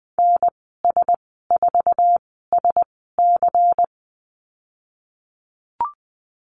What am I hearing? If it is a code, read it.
Morse code, heard as a keyed tone, about 20 wpm, spaced as for 15 wpm, 700 Hz. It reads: NS4SC
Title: 147 Lesson 24g - Letter y - Call Signs 20wpm